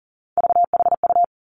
Morse code, heard as a keyed tone, about 41 wpm, 690 Hz. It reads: VHU
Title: Three Challenging Letter-Number Combinations 40wpm